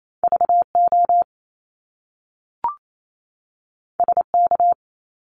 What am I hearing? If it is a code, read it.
Morse code, heard as a keyed tone, about 28 wpm, 700 Hz. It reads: VO SK
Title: Top Two Letter Combinations in Words 28wpm